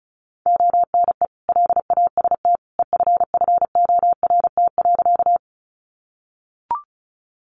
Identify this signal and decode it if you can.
Morse code, heard as a keyed tone, about 34 wpm, 700 Hz. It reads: ONE LAST EFFORT.
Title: Sentences from Top 1200 Words 35wpm